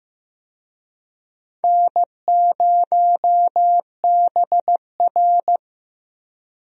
Morse code, heard as a keyed tone, about 16 wpm, 700 Hz. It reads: N0BR
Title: USA Call Signs - 4 Characters - Repeated 3x - 15wpm